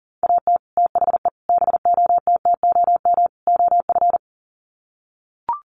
Morse code, heard as a keyed tone, about 38 wpm, 700 Hz. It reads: AT THE BOTTOM OF
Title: Sets of 4 Words 40wpm